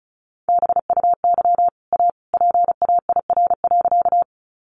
Morse code, heard as a keyed tone, about 34 wpm, 700 Hz. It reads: BUY A PAIR.
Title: Sentences from Top 1200 Words 35wpm